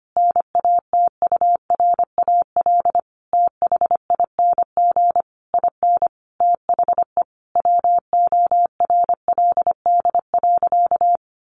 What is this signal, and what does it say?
Morse code, heard as a keyed tone, about 25 wpm, 700 Hz. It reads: NATURAL THING IN THE WORLD.
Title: Sentences from Top 700 Words 25wpm